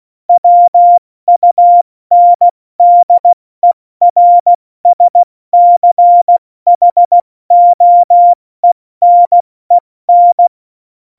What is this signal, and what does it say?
Morse code, heard as a keyed tone, about 16 wpm, spaced as for 14 wpm, 700 Hz. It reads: WUNDERSCHOENEN
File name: Morsekurs / Witze / Witz 04. September 2018, 70 F80